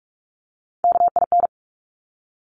Morse code, heard as a keyed tone, about 46 wpm, 700 Hz. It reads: KIN
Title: Top Three Letter Combinations in Words 45wpm